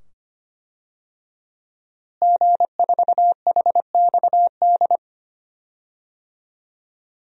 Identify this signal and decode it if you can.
Morse code, heard as a keyed tone, about 25 wpm, 700 Hz. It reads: G4HXD
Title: DXCC-100 Call Signs - 5 Characters - Repeated 2x - 25wpm